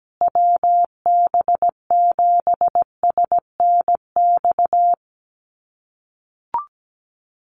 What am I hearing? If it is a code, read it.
Morse code, heard as a keyed tone, about 17 wpm, 700 Hz. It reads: WB7SNX
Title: USA Call Signs Repeated 3x 17wpm